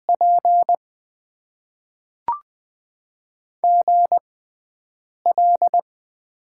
Morse code, heard as a keyed tone, about 20 wpm, spaced as for 7 wpm, 700 Hz. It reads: P GL
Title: Top 90 QSO Elements - 6x Character Spacing - 20wpm